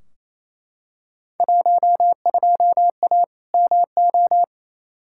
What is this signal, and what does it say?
Morse code, heard as a keyed tone, about 28 wpm, 700 Hz. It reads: 12A MO
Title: Field Day Exchanges Encore - 28wpm